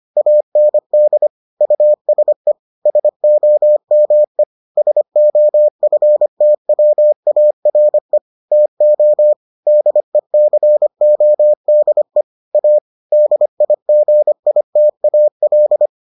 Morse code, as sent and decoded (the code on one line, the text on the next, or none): .- -. -.. / ..- ... / ... --- -- . / ... --- ..-. - .-- .- .-. / - --- / -.. -.-. --- -.. / .- / -.. .. --. .. - .- .-..
AND USE SOME SOFTWARE TO DECODE A DIGITAL